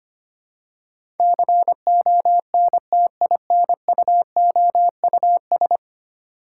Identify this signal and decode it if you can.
Morse code, heard as a keyed tone, about 25 wpm, 700 Hz. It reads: CONTINUOUS
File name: Ten Letter Words - 25wpm